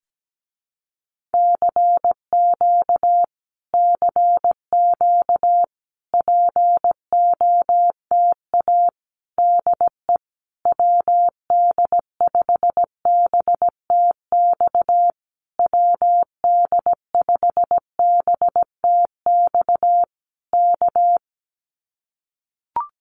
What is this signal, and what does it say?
Morse code, heard as a keyed tone, about 17 wpm, 700 Hz. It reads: CQ CQ POTA DE WD5BTX WD5BTX K
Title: Common QSO Phrases Encore 17wpm